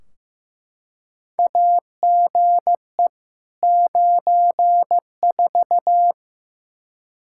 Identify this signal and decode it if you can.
Morse code, heard as a keyed tone, about 15 wpm, 700 Hz. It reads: AGE 94
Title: Practice QSO - Exchange 3 - More Info - 15wpm